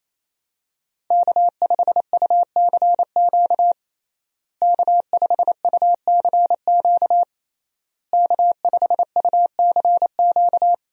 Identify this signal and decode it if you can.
Morse code, heard as a keyed tone, about 28 wpm, 700 Hz. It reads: K5UCQ K5UCQ K5UCQ
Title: USA Call Signs - 5 Characters - Repeated 3x - 28wpm